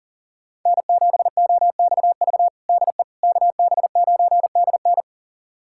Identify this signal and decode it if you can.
Morse code, heard as a keyed tone, about 40 wpm, 700 Hz. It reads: N7OXV DE KB9DN
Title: USA Callsign DE Callsign - 5 Characters - 40wpm